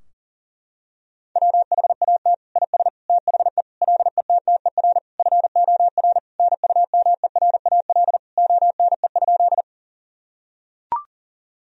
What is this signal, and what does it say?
Morse code, heard as a keyed tone, about 40 wpm, 700 Hz. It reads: WHAT IS THE LETTER FOR NUMERAL ONE?
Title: Sentences from Top 1000 Words - Review Part 2 - 40wpm